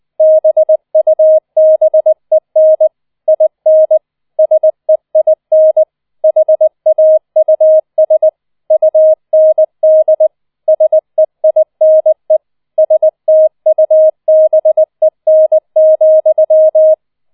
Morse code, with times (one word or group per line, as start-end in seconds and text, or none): 0.2-2.9 BUBEN
3.3-4.0 IN
4.4-5.8 SEIN
6.2-8.3 HAUS
8.7-10.3 UND
10.7-12.4 SEINE
12.8-16.9 STUBEN,